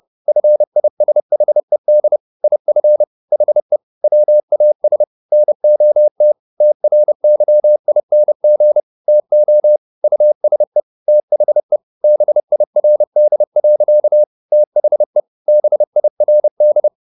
Morse code, heard as a keyed tone, about 30 wpm, 600 Hz. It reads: FISHED IF HE WAS NOT TRYING TO USE THE BIRD. THE BIRD
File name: the old man and the sea / the old man and the sea 1 30wpm